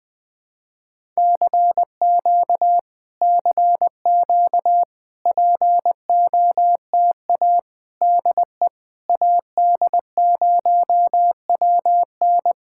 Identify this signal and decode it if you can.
Morse code, heard as a keyed tone, about 21 wpm, 700 Hz. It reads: CQ CQ POTA DE AD0WN
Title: POTA QSO Practice - 20wpm